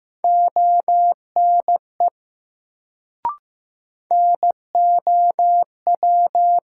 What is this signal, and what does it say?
Morse code, heard as a keyed tone, about 15 wpm, 700 Hz. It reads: ONE NOW
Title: Sentences from Top 100 Words - New Word Drill - 15wpm